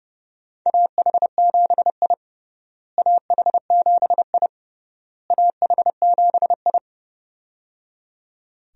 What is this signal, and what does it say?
Morse code, heard as a keyed tone, about 30 wpm, 700 Hz. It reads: AH7I AH7I AH7I
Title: USA Call Signs - 4 Characters - Repeated 3x - 30wpm